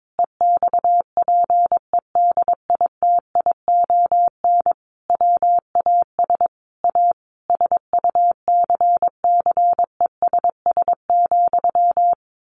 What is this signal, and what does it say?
Morse code, heard as a keyed tone, about 22 wpm, 700 Hz. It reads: EXPEDITION WAS A SUCCESS,
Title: A Sound Of Thunder 22wpm